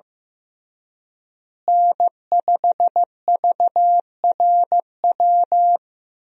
Morse code, heard as a keyed tone, about 15 wpm, 700 Hz. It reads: N5VRW
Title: USA Call Signs - 5 Characters - Repeated 3x - 15wpm